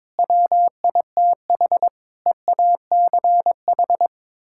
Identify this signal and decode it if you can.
Morse code, heard as a keyed tone, about 22 wpm, 700 Hz. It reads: WITH EACH
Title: Sentences from Top 1000 Words - Review Part 6 - 22wpm